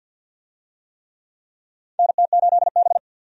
Morse code, heard as a keyed tone, about 53 wpm, 700 Hz. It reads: NT8B
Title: USA Call Signs - 4 Characters - 50wpm